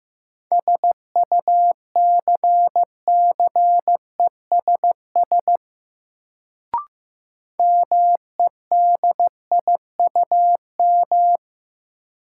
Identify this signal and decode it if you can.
Morse code, heard as a keyed tone, about 15 wpm, 700 Hz. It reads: SUCCESS MEDIUM
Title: Sentences from Top 900 Words - New Word Drill - 15wpm